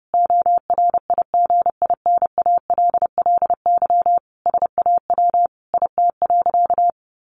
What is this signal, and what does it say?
Morse code, heard as a keyed tone, about 28 wpm, 700 Hz. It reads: ORIGINALLY SAW IT.
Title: Sentences from Top 1500 Words 30wpm